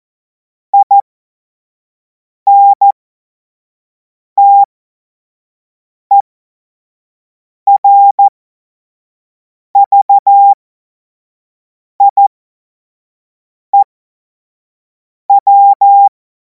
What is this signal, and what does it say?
Morse code, heard as a keyed tone, about 14 wpm, spaced as for 5 wpm, 800 Hz. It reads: INTERVIEW
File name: Morsekurs / Witze / Witz 02. Juni 2026 25 F70